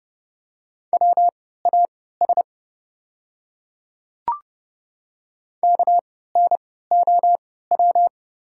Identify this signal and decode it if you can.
Morse code, heard as a keyed tone, about 30 wpm, spaced as for 17 wpm, 700 Hz. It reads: WAS KNOW
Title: 3x Character Spacing - Top 100 Words - 30wpm